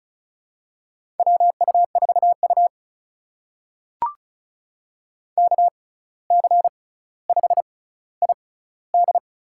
Morse code, heard as a keyed tone, about 36 wpm, spaced as for 17 wpm, 700 Hz. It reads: WU4U KC5ID
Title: USA Call Signs - 6x Character Spacing - 35wpm